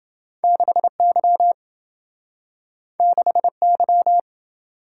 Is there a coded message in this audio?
Yes: Morse code from 0.4 to 4.2 s, heard as a keyed tone, about 27 wpm, 700 Hz. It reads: 6Y 6Y